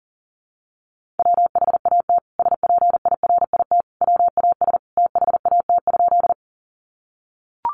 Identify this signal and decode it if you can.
Morse code, heard as a keyed tone, about 40 wpm, 700 Hz. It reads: WHAT SPIRIT WAS THAT?
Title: Sentences from Top 1400 Words 40wpm